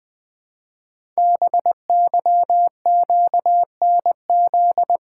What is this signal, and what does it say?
Morse code, heard as a keyed tone, about 20 wpm, 700 Hz. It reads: BYQNZ
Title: Code Groups - 5 Letters - 20wpm